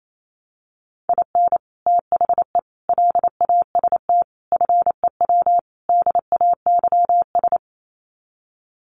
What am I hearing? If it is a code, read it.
Morse code, heard as a keyed tone, about 28 wpm, 700 Hz. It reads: IN THE LAST FEW DAYS
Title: Sets of 5 Words Encore2 28wpm